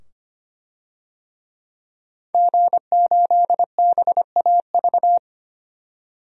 Morse code, heard as a keyed tone, about 25 wpm, 690 Hz. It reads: G8BAV
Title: DXCC-100 Call Signs - 5 Characters - 25wpm